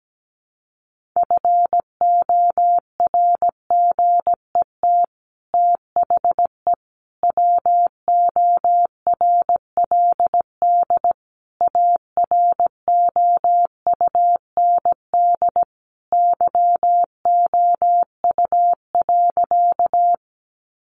Morse code, heard as a keyed tone, about 17 wpm, 700 Hz. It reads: FORGET THE WORLD AROUND YOU.
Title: Sentences from Top 1200 Words 17wpm